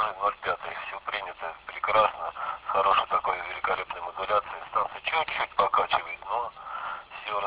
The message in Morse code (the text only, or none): none